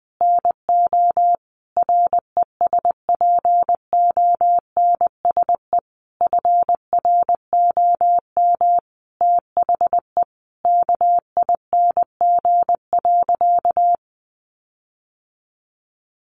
Morse code, heard as a keyed tone, about 20 wpm, 700 Hz. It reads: NO RESPONSE FROM THE KING.
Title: Sentences from Top 1400 Words 20wpm